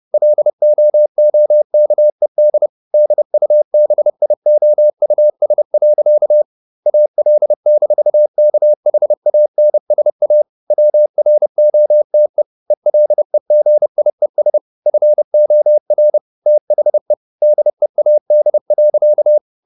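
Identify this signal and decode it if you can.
Morse code, heard as a keyed tone, about 30 wpm, 600 Hz. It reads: LOOKED DUBIOUS. AL-KHANSA WROTE ELEGIES FOR THE DEAD.
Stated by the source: Morse code at 30 wpm, the novel Never by Ken Follett in English, from CW med Gnister.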